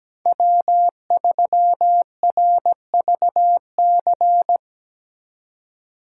Morse code, heard as a keyed tone, about 19 wpm, 700 Hz. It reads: W3RVC
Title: Speed Racing - USA 1x3 Call Signs 17wpm